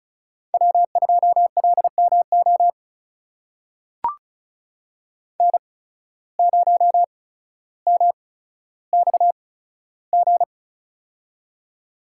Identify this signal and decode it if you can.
Morse code, heard as a keyed tone, about 36 wpm, spaced as for 15 wpm, 700 Hz. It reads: W2LMO N0MXG
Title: USA Call Signs - 5 Characters - 8x Character Spacing - 35wpm